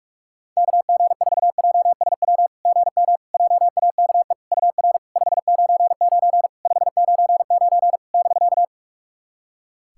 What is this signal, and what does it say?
Morse code, heard as a keyed tone, about 46 wpm, 690 Hz. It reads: KG4JSW GM JAKE UR 599 599 <BK>
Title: SOTA QSO Practice 45wpm